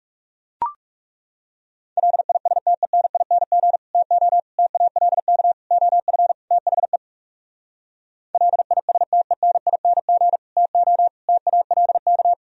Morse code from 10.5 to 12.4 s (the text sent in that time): TO TALK